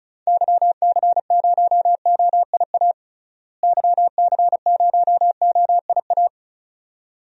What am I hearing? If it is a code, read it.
Morse code, heard as a keyed tone, about 36 wpm, 700 Hz. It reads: YC0OIA YC0OIA
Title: DXCC-100 Call Signs - 6 Characters - Repeated 2x - 35wpm